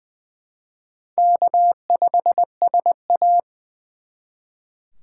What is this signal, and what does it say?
Morse code, heard as a keyed tone, about 24 wpm, 700 Hz. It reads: K5SA
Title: Speed Racing - USA 1x2 Call Signs 20wpm